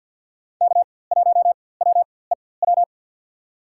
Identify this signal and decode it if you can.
Morse code, heard as a keyed tone, about 53 wpm, spaced as for 24 wpm, 700 Hz. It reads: K1WEW
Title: USA Call Signs - 4x Character Spacing - 50wpm